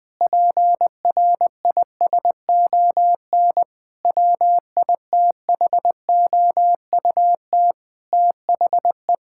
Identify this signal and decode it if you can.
Morse code, heard as a keyed tone, about 21 wpm, 700 Hz. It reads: PRISON WITHOUT THE
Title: Sets of 8 Words - 20wpm